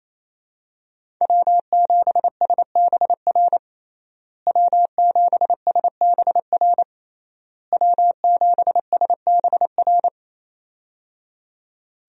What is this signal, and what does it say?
Morse code, heard as a keyed tone, about 28 wpm, 700 Hz. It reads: W7SBR W7SBR W7SBR
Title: USA Call Signs - 5 Characters - Repeated 3x - 28wpm